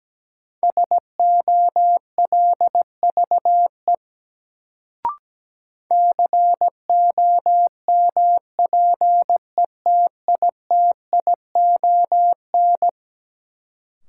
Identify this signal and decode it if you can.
Morse code, heard as a keyed tone, about 17 wpm, 700 Hz. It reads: SOLVE COMPETITION
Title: Sentences from Top 1000 Words - New Word Drill - 17wpm